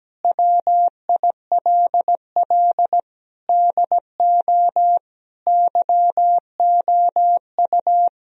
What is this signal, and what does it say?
Morse code, heard as a keyed tone, about 17 wpm, 700 Hz. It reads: WILL DO YOU
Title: Sentences from Top 1000 Words - Review Part 1 - 17wpm